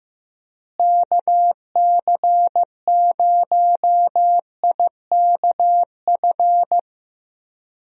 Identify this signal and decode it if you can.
Morse code, heard as a keyed tone, about 15 wpm, 700 Hz. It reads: KC0IKF
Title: USA Call Signs - 6 Characters - Repeated 3x - 15wpm